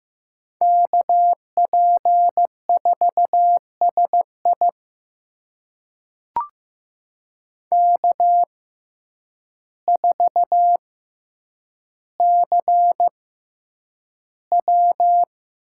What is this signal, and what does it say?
Morse code, heard as a keyed tone, about 15 wpm, spaced as for 8 wpm, 700 Hz. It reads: KP4SI K4CW
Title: USA Call Signs - 5 Characters - 6x Character Spacing - 15wpm